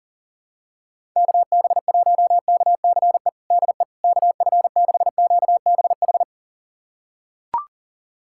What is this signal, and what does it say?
Morse code, heard as a keyed tone, about 37 wpm, 700 Hz. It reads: KB1KCE DE KF6QBH
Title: USA Callsign DE Callsign 40wpm